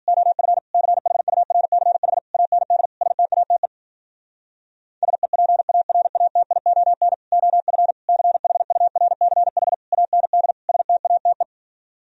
Morse code, heard as a keyed tone, about 52 wpm, 700 Hz. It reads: OF CHURCH AND STATE SEPARATION OF CHURCH AND STATE